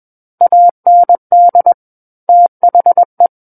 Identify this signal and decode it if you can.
Morse code, heard as a keyed tone, about 21 wpm, 700 Hz. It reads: AND THE
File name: Book One / 12-Chapter 12 - 21 WPM